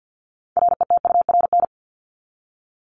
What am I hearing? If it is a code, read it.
Morse code, heard as a keyed tone, about 50 wpm, 700 Hz. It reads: RETURN
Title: Top 751-1000 Words 50wpm